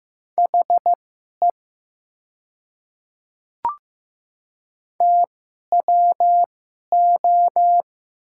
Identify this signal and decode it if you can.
Morse code, heard as a keyed tone, about 15 wpm, spaced as for 11 wpm, 690 Hz. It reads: HE TWO
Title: 2x Character Spacing - Top 100 Words - 15wpm